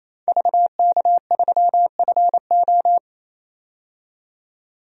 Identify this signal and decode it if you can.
Morse code, heard as a keyed tone, about 28 wpm, 700 Hz. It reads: VK3FO
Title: DXCC-100 Call Signs - 5 Characters - 28wpm